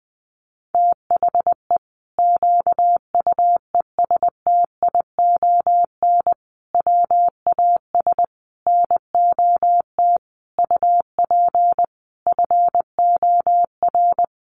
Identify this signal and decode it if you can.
Morse code, heard as a keyed tone, about 20 wpm, 700 Hz. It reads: THE QUESTION WAS NOT UP FOR
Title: Sentences from Top 1200 Words 20wpm